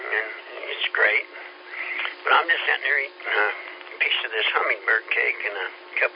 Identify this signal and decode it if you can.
Emergency Alert System (SAME) tones (header bursts), heard as harsh buzzing bursts, with speech. It is heard from 0.9 to 5.9 s.